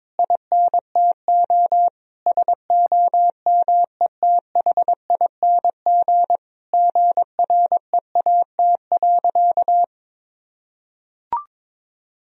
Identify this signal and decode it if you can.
Morse code, heard as a keyed tone, about 22 wpm, 700 Hz. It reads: INTO SOMETHING GREAT.
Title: Sentences from Top 1000 Words - Review Part 2 - 22wpm